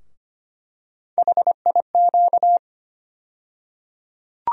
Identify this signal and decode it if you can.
Morse code, heard as a keyed tone, about 25 wpm, 700 Hz. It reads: HIQ
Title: USA Call Sign Postfixes - 3 Letters 25wpm